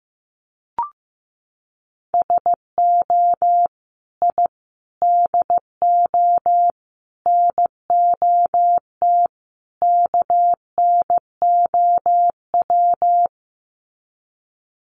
Morse code, heard as a keyed tone, about 15 wpm, 700 Hz. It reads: SO I DO NOT KNOW
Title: Sets of 5 Words Encore 15wpm